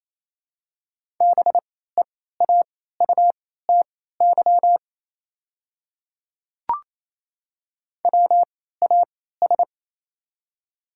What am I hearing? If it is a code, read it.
Morse code, heard as a keyed tone, about 28 wpm, spaced as for 16 wpm, 700 Hz. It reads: BEAUTY WAS
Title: 3x Character Spacing - Top 500 Words - 28wpm